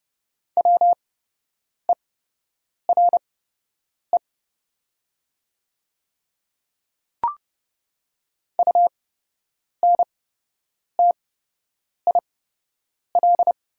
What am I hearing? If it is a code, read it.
Morse code, heard as a keyed tone, about 30 wpm, spaced as for 8 wpm, 700 Hz. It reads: WERE UNTIL
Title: 8x Character Spacing - Top 300 Words - 30wpm